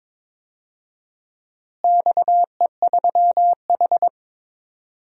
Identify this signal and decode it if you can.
Morse code, heard as a keyed tone, about 22 wpm, 700 Hz. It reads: XE3H
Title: DXCC-100 Call Signs - 4 Characters - Repeated 2x - 22wpm